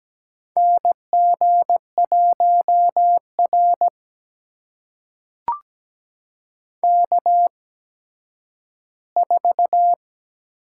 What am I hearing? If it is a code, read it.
Morse code, heard as a keyed tone, about 17 wpm, spaced as for 10 wpm, 700 Hz. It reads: NG1R K4
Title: USA Call Signs - 8x Character Spacing - 17wpm